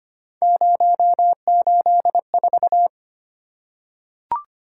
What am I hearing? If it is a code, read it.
Morse code, heard as a keyed tone, about 25 wpm, 700 Hz. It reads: 084
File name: Three Letter-Number Combinations - 25wpm